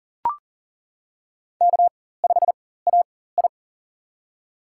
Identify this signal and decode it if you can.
Morse code, heard as a keyed tone, about 40 wpm, spaced as for 19 wpm, 700 Hz. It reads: K5AI
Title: USA Call Signs - 4 Characters - 4x Character Spacing - 40wpm